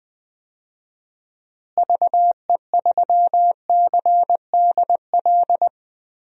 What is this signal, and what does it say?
Morse code, heard as a keyed tone, about 20 wpm, 700 Hz. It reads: VE3CDL
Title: DXCC-100 Call Signs - 6 Characters - Repeated 2x - 20wpm